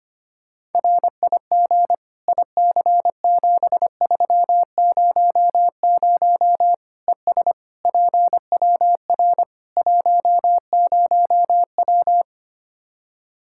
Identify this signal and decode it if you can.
Morse code, heard as a keyed tone, about 25 wpm, 700 Hz. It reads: RIG IC7300 ES PWR 10W
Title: K1USN SST QSO Practice 25wpm